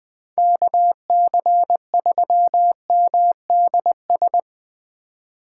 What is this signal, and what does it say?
Morse code, heard as a keyed tone, about 19 wpm, 700 Hz. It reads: KC3MDS
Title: Speed Racing - USA 2x3 Call Signs 17wpm